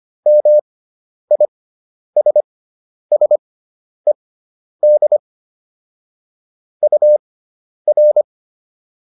Morse code, heard as a keyed tone, about 25 wpm, spaced as for 10 wpm, 600 Hz. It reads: MISSED UR